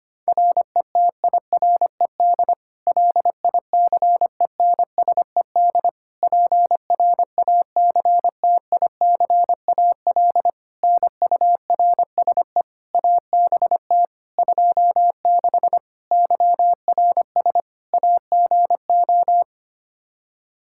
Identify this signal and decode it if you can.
Morse code, heard as a keyed tone, about 25 wpm, 700 Hz. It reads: RETIRED LICENSED PRACTICAL NURSE ABT 26 YRS AGO
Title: Practice QSO - Exchange 3 - More Info - 25wpm